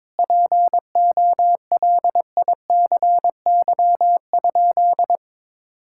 Morse code, heard as a keyed tone, about 22 wpm, 700 Hz. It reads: POLICY?
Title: Sentences from Top 1000 Words - Review Part 5 - 22wpm